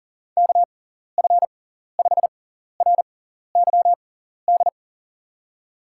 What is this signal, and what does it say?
Morse code, heard as a keyed tone, about 40 wpm, spaced as for 14 wpm, 700 Hz. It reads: KF5RYD
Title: USA Call Signs - 6x Character Spacing - 40wpm